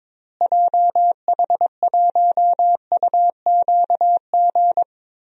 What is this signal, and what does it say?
Morse code, heard as a keyed tone, about 22 wpm, 700 Hz. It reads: JH1UQG
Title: DXCC-100 Call Signs - 6 Characters - Repeated 3x - 22wpm